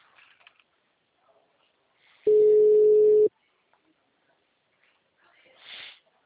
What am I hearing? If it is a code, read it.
Ringback tone (Europe, 425 Hz), with speech.